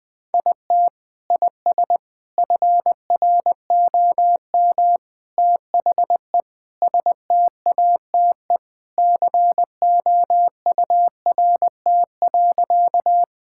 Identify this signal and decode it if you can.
Morse code, heard as a keyed tone, about 21 wpm, 700 Hz. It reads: IT IS FROM THE STATE COURT.
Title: Sentences from Top 1600 Words 20wpm